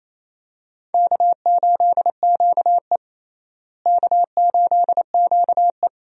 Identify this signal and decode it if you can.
Morse code, heard as a keyed tone, about 27 wpm, 700 Hz. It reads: K8QE K8QE
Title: USA Call Signs - 4 Characters - Repeated 2x - 28wpm